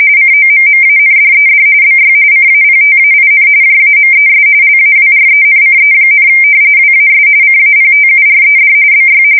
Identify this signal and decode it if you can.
RTTY, about 45.45 baud. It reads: COMMANDED BY CAPTAIN JAMES LAWRENCE, WAS CAPTURED BY
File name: fsk170 / usnavy2